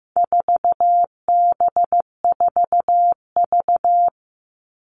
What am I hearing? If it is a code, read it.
Morse code, heard as a keyed tone, about 15 wpm, 700 Hz. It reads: 4B4V
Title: All Cut Numbers - 4-Digit 15wpm